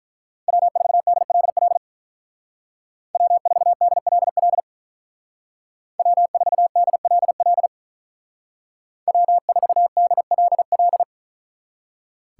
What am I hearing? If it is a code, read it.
Morse code, heard as a keyed tone, about 44 wpm, 700 Hz. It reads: W4DLL W4DLL W4DLL W4DLL
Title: Speed Racing - USA Call Signs 35wpm